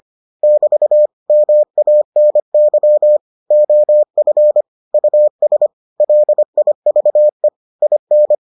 Morse code, heard as a keyed tone, about 25 wpm, 600 Hz. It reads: <BT> MANY OF US LIVE IN